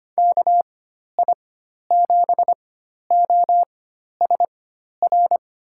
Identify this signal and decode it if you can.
Morse code, heard as a keyed tone, about 25 wpm, spaced as for 12 wpm, 700 Hz. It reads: KI7OSR
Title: USA Call Signs - 6 Characters - 4x Character Spacing - 25wpm